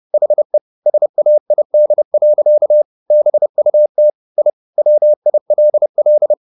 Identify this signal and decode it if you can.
Morse code, heard as a keyed tone, about 30 wpm, 600 Hz. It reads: HE SAID. BUT I WILL
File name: the old man and the sea / the old man and the sea 1 30wpm